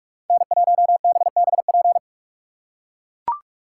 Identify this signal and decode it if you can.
Morse code, heard as a keyed tone, about 43 wpm, 700 Hz. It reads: N1BBP